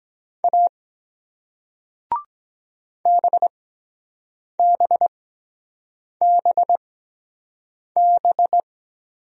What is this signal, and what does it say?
Morse code, heard as a keyed tone, about 23 wpm, spaced as for 7 wpm, 700 Hz. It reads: A BBBB